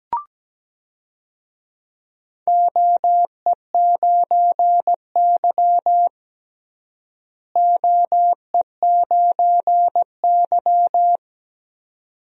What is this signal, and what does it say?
Morse code, heard as a keyed tone, about 17 wpm, 700 Hz. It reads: OE9Y OE9Y
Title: DXCC-100 Call Signs - 4 Characters - Repeated 2x - 17wpm